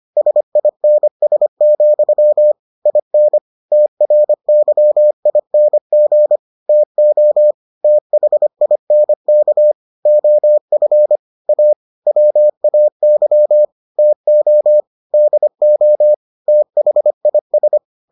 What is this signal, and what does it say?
Morse code, heard as a keyed tone, about 25 wpm, 600 Hz. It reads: SINS, IN TRYING TO THINK OF A WAY TO DO THIS